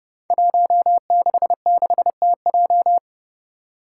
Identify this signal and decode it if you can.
Morse code, heard as a keyed tone, about 30 wpm, 700 Hz. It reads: 166TJ
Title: Five Challenging Letter-Number Combinations 30wpm